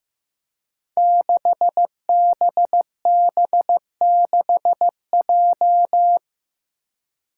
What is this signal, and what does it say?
Morse code, heard as a keyed tone, about 15 wpm, 700 Hz. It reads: 6BB6J
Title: Five Challenging Letter-Number Combinations 15wpm